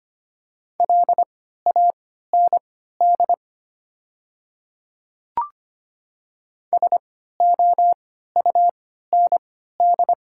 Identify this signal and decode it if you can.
Morse code, heard as a keyed tone, about 25 wpm, spaced as for 14 wpm, 700 Hz. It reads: LAND SOUND
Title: 3x Character Spacing - Top 200 Words - 25wpm